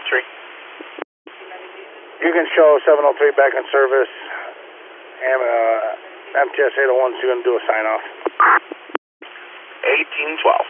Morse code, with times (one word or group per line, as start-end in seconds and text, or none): none